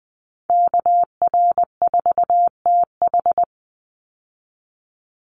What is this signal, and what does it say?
Morse code, heard as a keyed tone, about 20 wpm, 700 Hz. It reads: KR4TH
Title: Call Signs - 20wpm